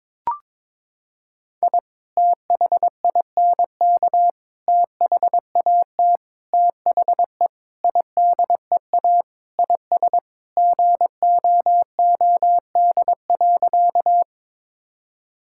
Morse code, heard as a keyed tone, about 22 wpm, 700 Hz. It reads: I THINK THAT THE IDEA IS GOOD.
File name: practice / 22wpm / Sentences with 4-Word Phrases - 22wpm